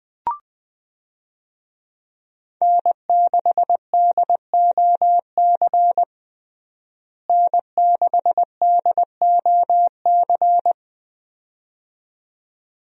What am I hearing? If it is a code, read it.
Morse code, heard as a keyed tone, about 20 wpm, 700 Hz. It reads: N6DOC N6DOC
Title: USA Call Signs - 5 Characters - Repeated 2x - 20wpm